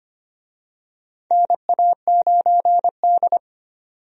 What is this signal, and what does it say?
Morse code, heard as a keyed tone, about 25 wpm, 700 Hz. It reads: NA9D
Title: USA Call Signs - 4 Characters - 25wpm